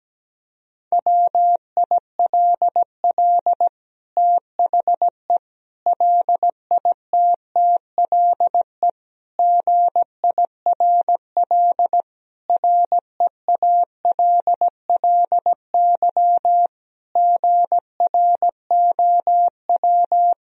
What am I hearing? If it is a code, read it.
Morse code, heard as a keyed tone, about 17 wpm, 700 Hz. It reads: WILL THE LITTLE GIRL REALLY GROW
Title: Sentences from Top 1000 Words - Review Part 1 - 17wpm